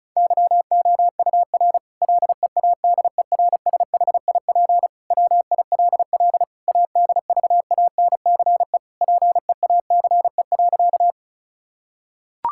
Morse code, heard as a keyed tone, about 34 wpm, 700 Hz. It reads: YOUR LEADERSHIP WILL ADVANCE PEACE.
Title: Sentences from Top 1600 Words 35wpm